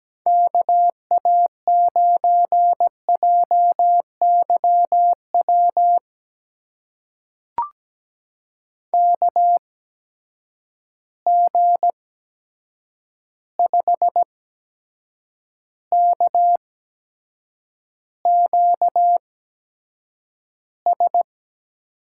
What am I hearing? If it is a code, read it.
Morse code, heard as a keyed tone, about 17 wpm, spaced as for 7 wpm, 700 Hz. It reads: KA9JYW KG5KQS
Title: USA Call Signs - 6 Characters - 8x Character Spacing - 17wpm